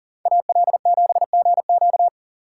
Morse code, heard as a keyed tone, about 38 wpm, 700 Hz. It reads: AL7GQ
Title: Speed Racing - USA Call Signs 40wpm